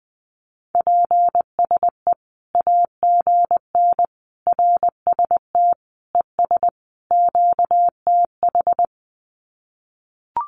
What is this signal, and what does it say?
Morse code, heard as a keyed tone, about 20 wpm, 700 Hz. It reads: PSE AGN RST ES QTH
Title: Common QSO Phrases 20wpm